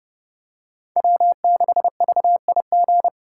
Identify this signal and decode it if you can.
Morse code, heard as a keyed tone, about 31 wpm, 700 Hz. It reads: W6VIG
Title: USA Call Signs - 5 Characters - 4x Character Spacing - 30wpm